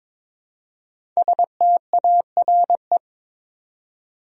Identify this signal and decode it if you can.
Morse code, heard as a keyed tone, about 22 wpm, 690 Hz. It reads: STARE
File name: 5 Letter Words - First 3 Letters Given - 22wpm